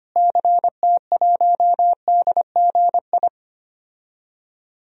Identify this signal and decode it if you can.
Morse code, heard as a keyed tone, about 25 wpm, 700 Hz. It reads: CT1DGI
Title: DXCC-100 Call Signs - 6 Characters - Repeated 2x - 25wpm